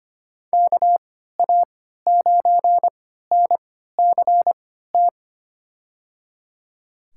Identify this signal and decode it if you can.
Morse code, heard as a keyed tone, about 25 wpm, spaced as for 14 wpm, 700 Hz. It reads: KA9NCT
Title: USA Call Signs - 6 Characters - 3x Character Spacing - 25wpm